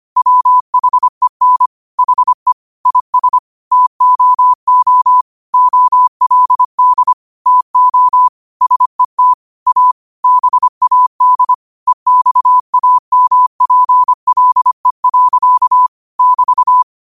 Morse code, heard as a keyed tone, about 25 wpm, 1000 Hz. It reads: WHEN HE IS TOO OLD TO SET A BAD EXAMPLE. <BT>
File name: qotdfiles / 25wpm